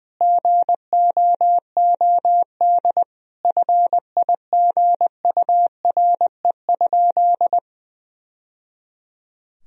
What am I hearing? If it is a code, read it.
Morse code, heard as a keyed tone, about 20 wpm, 700 Hz. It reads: GOOD FIGURE?
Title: Sentences from Top 1000 Words - Review Part 2 - 20wpm